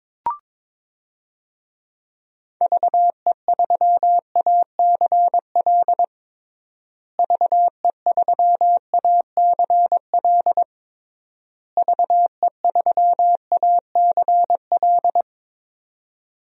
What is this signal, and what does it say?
Morse code, heard as a keyed tone, about 22 wpm, 700 Hz. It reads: VE3ACL VE3ACL VE3ACL
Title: DXCC-100 Call Signs - 6 Characters - Repeated 3x - 22wpm